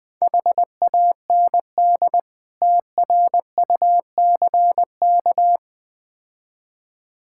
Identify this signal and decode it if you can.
Morse code, heard as a keyed tone, about 20 wpm, 700 Hz. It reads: HAND TRUCK
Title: Difficult Words - Speed-Racing with Hints - 20wpm